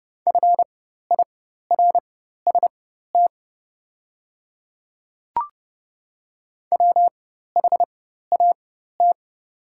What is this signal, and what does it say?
Morse code, heard as a keyed tone, about 30 wpm, spaced as for 14 wpm, 700 Hz. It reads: FIRST WHAT
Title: 4x Character Spacing - Top 100 Words - 30wpm